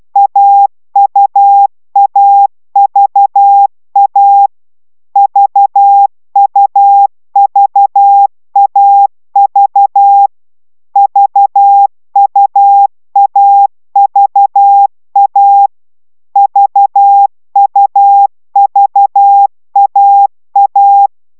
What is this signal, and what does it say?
Morse code, heard as a keyed tone, about 12 wpm, 800 Hz. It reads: AUAVA VUVAV VUAVA VUVAA